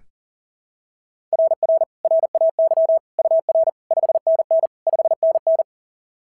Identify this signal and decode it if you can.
Morse code, heard as a keyed tone, about 39 wpm, 690 Hz. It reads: RR RAY UR 5NN 5NN
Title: SOTA QSO Practice 40wpm